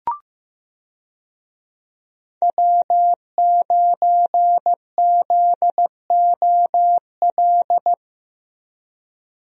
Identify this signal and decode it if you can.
Morse code, heard as a keyed tone, about 15 wpm, 700 Hz. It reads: W9ZOL
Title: USA Call Signs - 5 Characters - 15wpm